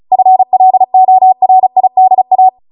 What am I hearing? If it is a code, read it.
Morse code, heard as a keyed tone, about 36 wpm, 750 Hz. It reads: FLORIDA